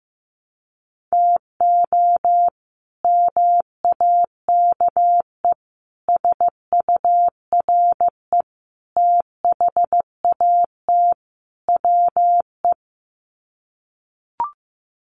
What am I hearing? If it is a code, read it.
Morse code, heard as a keyed tone, about 15 wpm, 700 Hz. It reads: TO MAKE SURE THAT WE
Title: Sets of 5 Words Encore 15wpm